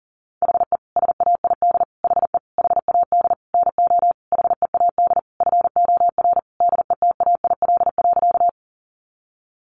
Morse code, heard as a keyed tone, about 40 wpm, 700 Hz. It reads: HE SAID HE HAD NO HEAD FOR DETAIL.
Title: Sentences from Top 1500 Words 40wpm